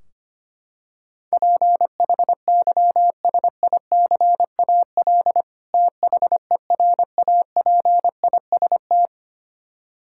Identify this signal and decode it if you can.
Morse code, heard as a keyed tone, about 25 wpm, 700 Hz. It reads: PHYSICAL THERAPIST
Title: Practice QSO - Exchange 3 - More Info - 25wpm